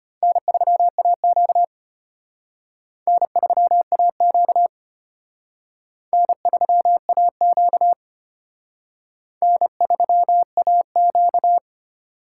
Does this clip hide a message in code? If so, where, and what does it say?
Yes: Morse code from 0.2 to 11.6 s, heard as a keyed tone, about 33 wpm, 700 Hz. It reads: N3AQ N3AQ N3AQ N3AQ